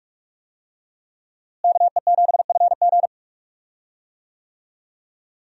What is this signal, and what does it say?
Morse code, heard as a keyed tone, about 42 wpm, 700 Hz. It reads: KE7FG
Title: USA Call Signs - 5 Characters - Repeated 2x - 45wpm